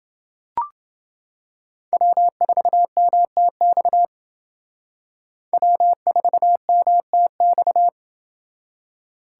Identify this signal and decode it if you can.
Morse code, heard as a keyed tone, about 28 wpm, 700 Hz. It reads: W4MTX W4MTX